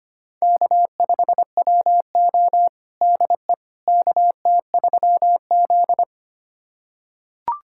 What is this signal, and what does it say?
Morse code, heard as a keyed tone, about 25 wpm, 700 Hz. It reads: K5WO DE KT3Z
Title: USA Callsign DE Callsign - 4 Characters - 25wpm